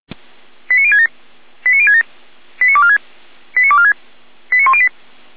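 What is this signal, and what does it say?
Five-tone selective call (ZVEI), heard as quick runs of five tones, tones of about 70 ms. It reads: 89067 89067 89246 89246 89108
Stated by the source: Five-tone ZVEI selective call with code 89099, received on an SDR.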